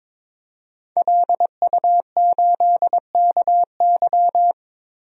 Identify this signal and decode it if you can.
Morse code, heard as a keyed tone, about 22 wpm, 700 Hz. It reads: LU8KY
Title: DXCC-100 Call Signs - 5 Characters - 22wpm